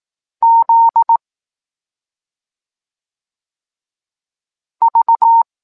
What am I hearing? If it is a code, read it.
Morse code, heard as a keyed tone, about 18 wpm, 920 Hz. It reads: Z V
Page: Tekst 23 – Doskonalenie odbioru poznanych liter